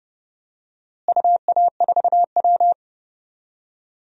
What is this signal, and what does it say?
Morse code, heard as a keyed tone, about 31 wpm, 700 Hz. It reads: UA4W
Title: DXCC-100 Call Signs - 4 Characters - Repeated 3x - 30wpm